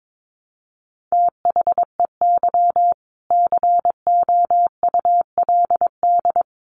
Morse code, heard as a keyed tone, about 22 wpm, 700 Hz. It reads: THEY COULD
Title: Sentences from Top 1000 Words 22wpm